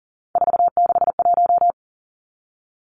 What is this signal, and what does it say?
Morse code, heard as a keyed tone, about 39 wpm, 700 Hz. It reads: 461
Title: Three Challenging Letter-Number Combinations 40wpm